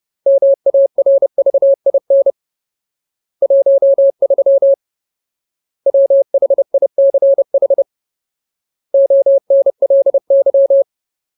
Morse code, heard as a keyed tone, about 30 wpm, 550 Hz. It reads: MARVIN 13 WHICH ONLY
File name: Ultimate Word List / 30WPM Ultimate